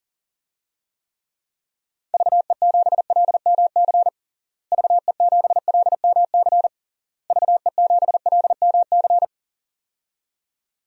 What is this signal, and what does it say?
Morse code, heard as a keyed tone, about 40 wpm, 700 Hz. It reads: VE7LMC VE7LMC VE7LMC
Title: DXCC-100 Call Signs - 6 Characters - Repeated 3x - 40wpm